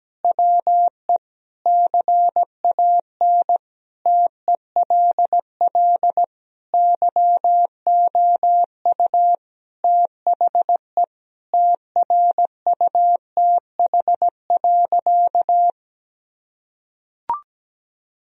Morse code, heard as a keyed tone, about 17 wpm, 700 Hz. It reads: WE CAN TELL YOU THE TRUTH.
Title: Sentences from Top 1000 Words - Review Part 2 - 17wpm